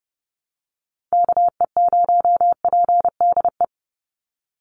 Morse code, heard as a keyed tone, about 30 wpm, 700 Hz. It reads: KE0PDE
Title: USA Call Signs 30wpm